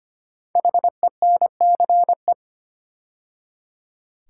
Morse code, heard as a keyed tone, about 25 wpm, 700 Hz. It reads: HENCE
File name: 5 Letter Words - First 3 Letters Given - 25wpm